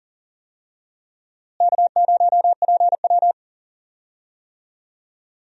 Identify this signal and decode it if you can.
Morse code, heard as a keyed tone, about 40 wpm, 700 Hz. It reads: K0PW
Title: USA Call Signs - 4 Characters - Repeated 3x - 40wpm